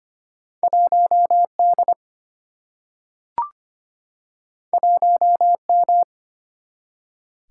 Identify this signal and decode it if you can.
Morse code, heard as a keyed tone, about 25 wpm, 700 Hz. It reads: 1D 1M
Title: Single-Letter Single-Number 25wpm